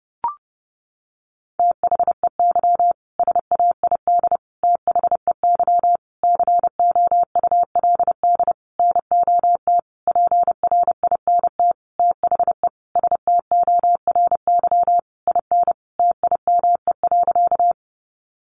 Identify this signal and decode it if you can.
Morse code, heard as a keyed tone, about 28 wpm, 700 Hz. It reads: THEY SAID THEY COULD NOT PRINT THE STORY IN TIME.
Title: Sentences from Top 1400 Words 30wpm